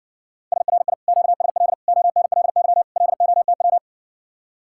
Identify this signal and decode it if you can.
Morse code, heard as a keyed tone, about 60 wpm, 700 Hz. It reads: SRI QSL ONLY LOTW